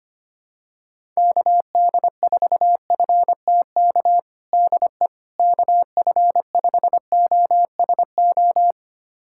Morse code, heard as a keyed tone, about 25 wpm, 700 Hz. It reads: KD4FTK DE KF5OSO
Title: USA Callsign DE Callsign 25wpm